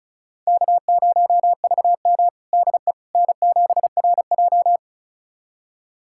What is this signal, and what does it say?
Morse code, heard as a keyed tone, about 36 wpm, 700 Hz. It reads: K0VM DE N7RJ
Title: USA Callsign DE Callsign - 4 Characters - 35wpm